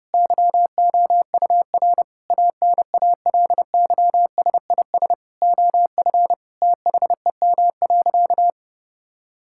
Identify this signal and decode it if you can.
Morse code, heard as a keyed tone, about 30 wpm, 700 Hz. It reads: YOUR ANALYSIS OF THEM.
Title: Sentences from Top 1000 Words - Review Part 5 - 30wpm